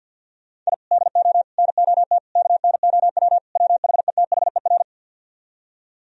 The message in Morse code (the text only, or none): I DO NOT KNOW WHETHER